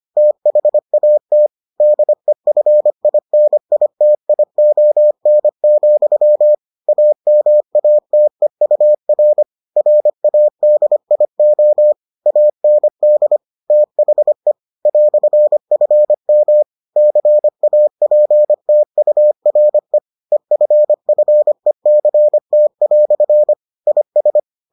Morse code, heard as a keyed tone, about 25 wpm, 600 Hz. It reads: THAT DEFINITION, AMATEUR RADIO AND THE "FM CAPTURE EFFECT" IS